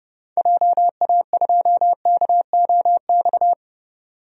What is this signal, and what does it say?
Morse code, heard as a keyed tone, about 30 wpm, 700 Hz. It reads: JA2KOX
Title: DXCC-100 Call Signs - 6 Characters - 30wpm